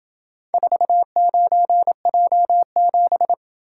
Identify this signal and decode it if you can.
Morse code, heard as a keyed tone, about 26 wpm, 700 Hz. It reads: 49J7